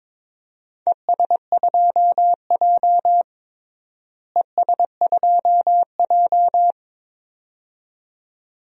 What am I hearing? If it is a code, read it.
Morse code, heard as a keyed tone, about 21 wpm, 700 Hz. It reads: ES2J ES2J
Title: DXCC-100 Call Signs - 4 Characters - Repeated 2x - 22wpm